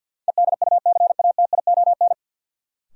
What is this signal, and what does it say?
Morse code, heard as a keyed tone, about 46 wpm, 700 Hz. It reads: EDUCATION
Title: Sentences from Top 1100 Words - New Word Drill Alt - 50wpm